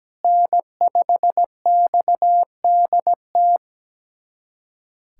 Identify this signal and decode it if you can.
Morse code, heard as a keyed tone, about 17 wpm, 700 Hz. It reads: N5XDT
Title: USA Callsign DE Callsign - 5 Characters - 17wpm